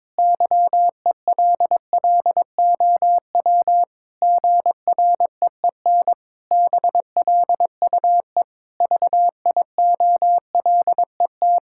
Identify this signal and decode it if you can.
Morse code, heard as a keyed tone, about 22 wpm, 700 Hz. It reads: YELLOW GREEN BLUE VIOLET
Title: Pangram Challenge v2 - 22wpm